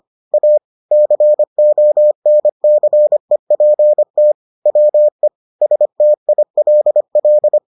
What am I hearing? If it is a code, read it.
Morse code, heard as a keyed tone, about 25 wpm, 600 Hz. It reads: A CONCEPT WE STILL